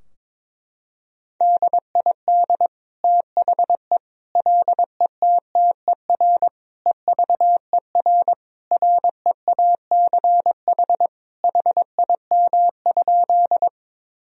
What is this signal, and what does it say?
Morse code, heard as a keyed tone, about 22 wpm, 700 Hz. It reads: DID THE LETTER EVER REACH HIM?
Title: Sentences from Top 1000 Words - Review Part 2 - 22wpm